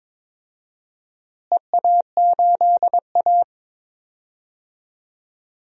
Morse code, heard as a keyed tone, about 22 wpm, 700 Hz. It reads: EA8A
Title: DXCC-100 Call Signs - 4 Characters - Repeated 3x - 22wpm